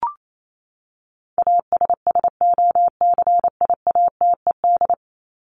Morse code, heard as a keyed tone, about 28 wpm, 700 Hz. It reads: ASSOCIATED
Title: ED Suffix - Top 100 Words with Base Words 28wpm